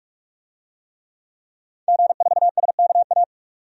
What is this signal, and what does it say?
Morse code, heard as a keyed tone, about 44 wpm, 700 Hz. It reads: G4SKA
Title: DXCC-100 Call Signs - 5 Characters - 45wpm